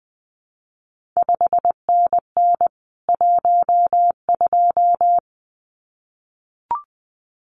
Morse code, heard as a keyed tone, about 20 wpm, 700 Hz. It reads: 5NN 12
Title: CQ WW DX Contest Exchange Encore - 20wpm